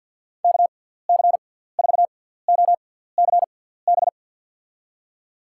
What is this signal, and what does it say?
Morse code, heard as a keyed tone, about 53 wpm, spaced as for 17 wpm, 700 Hz. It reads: KC4OCB
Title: USA Call Signs - 6x Character Spacing - 50wpm